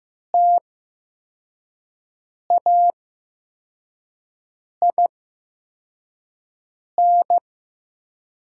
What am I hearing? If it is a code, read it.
Morse code, heard as a keyed tone, about 15 wpm, spaced as for 4 wpm, 700 Hz. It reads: TAIN
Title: 8x Character Spacing - Top 300 Words - 15wpm